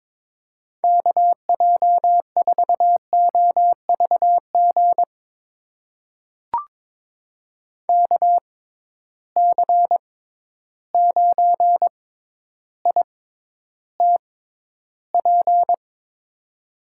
Morse code, heard as a keyed tone, about 22 wpm, spaced as for 11 wpm, 700 Hz. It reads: KJ4OVG KC9ITP
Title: USA Call Signs - 6 Characters - 6x Character Spacing - 22wpm